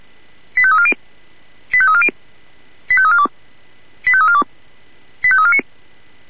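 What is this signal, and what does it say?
Five-tone selective call (ZVEI), heard as quick runs of five tones, tones of about 70 ms. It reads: 85340 85340 85342 85342 85349
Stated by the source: ZVEI five-tone call received by SDR.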